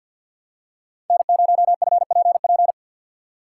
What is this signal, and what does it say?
Morse code, heard as a keyed tone, about 47 wpm, 700 Hz. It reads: N0FPP
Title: USA Call Signs - 5 Characters - 3x Character Spacing - 50wpm